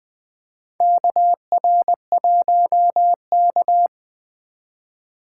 Morse code, heard as a keyed tone, about 20 wpm, 700 Hz. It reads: KR1K
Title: USA Call Signs - 4 Characters - 3x Character Spacing - 20wpm